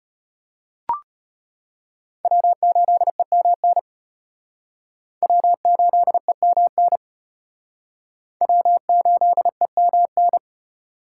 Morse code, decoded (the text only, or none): W8EMN W8EMN W8EMN